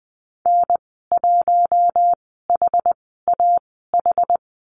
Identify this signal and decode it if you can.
Morse code, heard as a keyed tone, about 20 wpm, spaced as for 14 wpm, 700 Hz. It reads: N1HAH
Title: 193 Lesson 31g - Letter v - Call Signs 20wpm